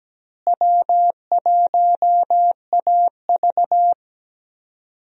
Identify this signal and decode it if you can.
Morse code, heard as a keyed tone, about 17 wpm, 700 Hz. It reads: W1AV
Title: USA Call Signs - 4 Characters - Repeated 2x - 17wpm